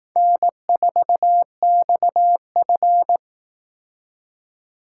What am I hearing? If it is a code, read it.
Morse code, heard as a keyed tone, about 18 wpm, 700 Hz. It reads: N4XF
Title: Speed Racing - USA 1x2 Call Signs 15wpm